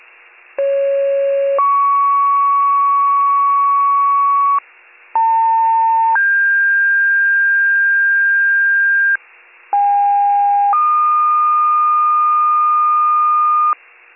Two-tone page, heard as a one-second tone, then a three-second tone, about 550 then 1120 Hz, about 900 then 1600 Hz, about 810 then 1220 Hz. It is heard from 0.6 to 13.7 s.